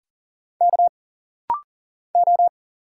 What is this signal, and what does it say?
Morse code, heard as a keyed tone, about 39 wpm, 700 Hz. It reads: K O
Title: Single Letter-Number Alt - Rapid-Fire 40wpm